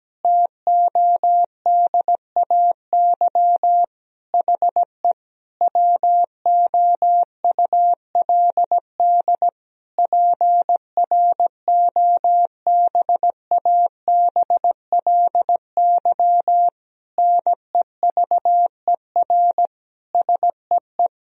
Morse code, heard as a keyed tone, about 17 wpm, 700 Hz. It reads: TODAY HE WOULD PROBABLY NEVER SEE
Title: Sentences from Top 1000 Words - Review Part 4 - 17wpm